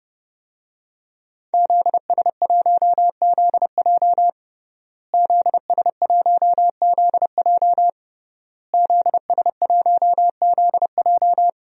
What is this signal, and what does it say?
Morse code, heard as a keyed tone, about 30 wpm, 700 Hz. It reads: ZS1ZJ ZS1ZJ ZS1ZJ
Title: DXCC-100 Call Signs - 5 Characters - Repeated 3x - 30wpm